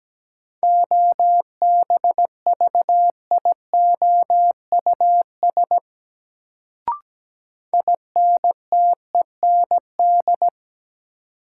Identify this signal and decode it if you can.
Morse code, heard as a keyed tone, about 17 wpm, 700 Hz. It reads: OBVIOUS INTEND
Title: Sentences from Top 1600 Words - New Word Drill Alt - 17wpm